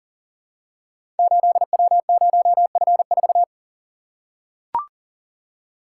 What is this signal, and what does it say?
Morse code, heard as a keyed tone, about 39 wpm, 700 Hz. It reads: 8W0F4
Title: Code Groups - 5 Letter:Numbers - 40wpm